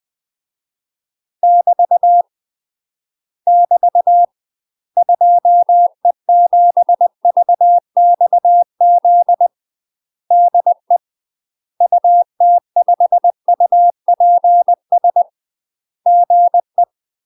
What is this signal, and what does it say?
Morse code, heard as a keyed tone, about 20 wpm, 700 Hz. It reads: <BT> <BT> 2E7VXZ DE UT5UPS GE
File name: UK MASTER QSOs TRAINING / 20WPM / UK MASTER QSOs 0063